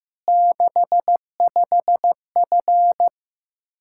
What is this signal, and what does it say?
Morse code, heard as a keyed tone, about 15 wpm, 700 Hz. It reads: 65F
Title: Practice QSO - Exchange 2 - Info - 15wpm